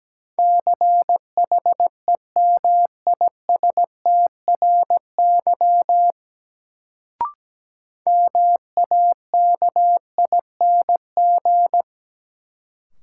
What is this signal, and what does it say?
Morse code, heard as a keyed tone, about 17 wpm, 700 Hz. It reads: CHEMISTRY MAKING
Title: Sentences from Top 1000 Words - Review Part 6 - New Word Drill - 17wpm